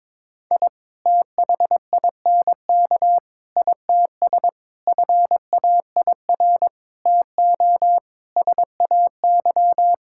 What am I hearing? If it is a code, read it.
Morse code, heard as a keyed tone, about 22 wpm, 700 Hz. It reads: I THINK ITS FAIR TO SAY
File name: practice / 22wpm / Sets of 6 Words - 22wpm